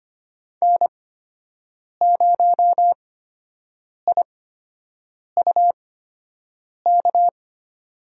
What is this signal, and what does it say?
Morse code, heard as a keyed tone, about 25 wpm, spaced as for 7 wpm, 700 Hz. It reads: N0IUK
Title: USA Call Signs - 5 Characters - 8x Character Spacing - 25wpm